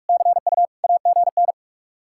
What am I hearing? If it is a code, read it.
Morse code, heard as a keyed tone, about 66 wpm, spaced as for 55 wpm, 700 Hz. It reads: CU AGN
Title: Speed-Racing Common QSO Phrases - 45wpm